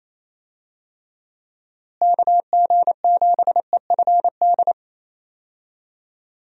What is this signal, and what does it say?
Morse code, heard as a keyed tone, about 27 wpm, 700 Hz. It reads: KG7EFD
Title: USA Call Signs - 6 Characters - Repeated 3x - 28wpm